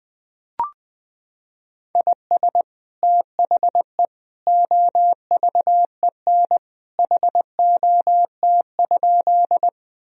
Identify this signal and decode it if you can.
Morse code, heard as a keyed tone, about 21 wpm, 700 Hz. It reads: IS THE OVEN HOT?
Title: Sentences from Top 1000 Words - Review Part 4 - 20wpm